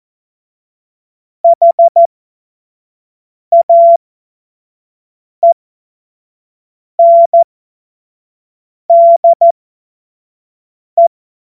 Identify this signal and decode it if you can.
Morse code, heard as a keyed tone, about 14 wpm, spaced as for 5 wpm, 680 Hz. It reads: HAENDE